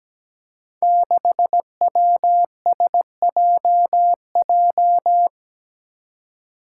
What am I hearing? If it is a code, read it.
Morse code, heard as a keyed tone, about 17 wpm, 700 Hz. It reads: 6WSJJ
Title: Five Challenging Letter-Number Combinations 17wpm